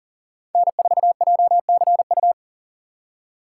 Morse code, heard as a keyed tone, about 40 wpm, 700 Hz. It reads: N4JCU
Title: USA Call Signs - 5 Characters - 3x Character Spacing - 40wpm